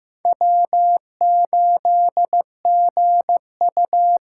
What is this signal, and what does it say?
Morse code, heard as a keyed tone, about 15 wpm, 700 Hz. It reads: W8GU
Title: K1USN SST QSO Practice - 15wpm